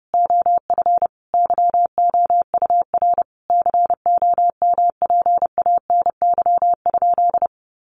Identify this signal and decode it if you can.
Morse code, heard as a keyed tone, about 30 wpm, 700 Hz. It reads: OF YOUR COMPANY?
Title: Sentences from Top 600 Words 30wpm